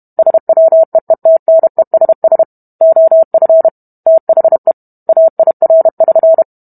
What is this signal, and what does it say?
Morse code, heard as a keyed tone, about 32 wpm, 650 Hz. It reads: SWEETNESS OF THE AIR<SN>